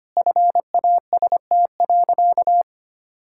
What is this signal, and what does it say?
Morse code, heard as a keyed tone, about 25 wpm, 700 Hz. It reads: FAST.
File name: practice / 25wpm / Sentences with 2-Word Phrases - 25wpm